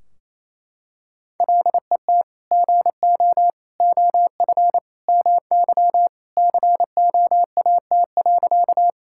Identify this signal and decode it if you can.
Morse code, heard as a keyed tone, about 28 wpm, 700 Hz. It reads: LET GO OF MY COAT.
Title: Sentences from Top 1600 Words 28wpm